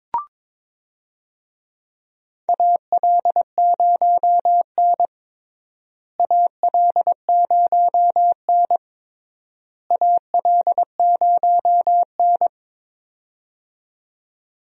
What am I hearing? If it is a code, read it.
Morse code, heard as a keyed tone, about 22 wpm, 700 Hz. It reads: AL0N AL0N AL0N
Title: USA Call Signs - 4 Characters - Repeated 3x - 22wpm